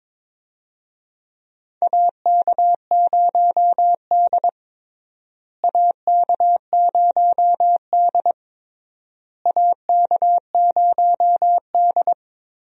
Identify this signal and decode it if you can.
Morse code, heard as a keyed tone, about 22 wpm, 700 Hz. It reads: AK0D AK0D AK0D
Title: USA Call Signs - 4 Characters - Repeated 3x - 22wpm